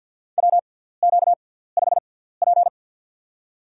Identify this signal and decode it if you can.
Morse code, heard as a keyed tone, about 52 wpm, spaced as for 17 wpm, 700 Hz. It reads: WQ5P
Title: USA Call Signs - 6x Character Spacing - 50wpm